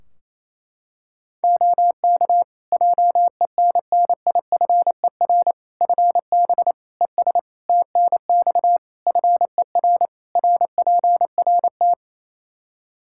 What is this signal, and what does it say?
Morse code, heard as a keyed tone, about 28 wpm, 700 Hz. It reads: OK JENNIFER FB ES TNX FER RPRT
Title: Practice QSO - Exchange 2 - Info - 28wpm